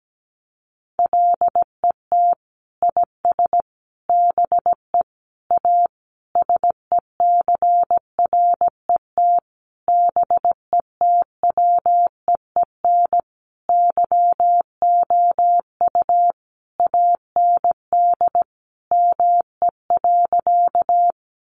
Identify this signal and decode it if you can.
Morse code, heard as a keyed tone, about 17 wpm, 700 Hz. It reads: LET IS BE A SECRET BETWEEN YOU AND ME.
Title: Sentences from Top 1200 Words 17wpm